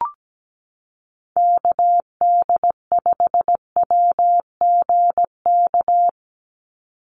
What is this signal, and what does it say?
Morse code, heard as a keyed tone, about 17 wpm, 700 Hz. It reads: KD5WGK
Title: USA Call Signs 17wpm